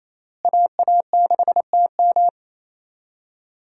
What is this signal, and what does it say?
Morse code, heard as a keyed tone, about 27 wpm, 700 Hz. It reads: AA6TM
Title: USA Call Signs - 5 Characters - Repeated 3x - 28wpm